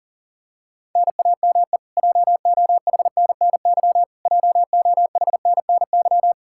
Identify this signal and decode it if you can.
Morse code, heard as a keyed tone, about 40 wpm, 700 Hz. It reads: NAME JOHNNY JOHNNY
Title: Practice QSO - Exchange 1 - Beginning - 40wpm